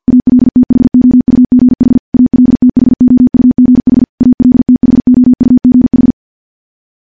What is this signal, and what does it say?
Morse code, heard as a keyed tone, about 50 wpm, 250 Hz. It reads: ALTHOUGH ALTHOUGH ALTHOUGH
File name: morse / morse-training-50WPM